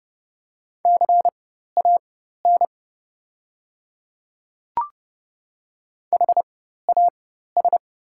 Morse code, heard as a keyed tone, about 31 wpm, spaced as for 14 wpm, 690 Hz. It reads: CAN HAS
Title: 4x Character Spacing - Top 100 Words - 30wpm